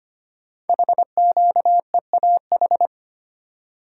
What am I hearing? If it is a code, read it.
Morse code, heard as a keyed tone, about 25 wpm, 700 Hz. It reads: HQEAH
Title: Code Groups - 5 Letters - 25wpm